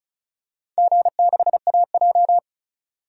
Morse code, heard as a keyed tone, about 36 wpm, 700 Hz. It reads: G6AJ
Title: DXCC-100 Call Signs - 4 Characters - 35wpm